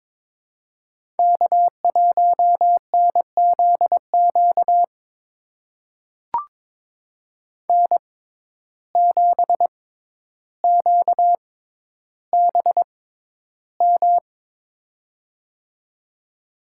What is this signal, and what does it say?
Morse code, heard as a keyed tone, about 21 wpm, spaced as for 11 wpm, 700 Hz. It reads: K1NZQ N7QBM
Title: USA Call Signs - 5 Characters - 6x Character Spacing - 22wpm